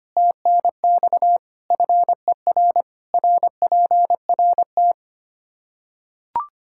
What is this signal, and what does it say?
Morse code, heard as a keyed tone, about 25 wpm, 700 Hz. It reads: TNX FER RPRT
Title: Practice QSO - Exchange 2 - Info - 25wpm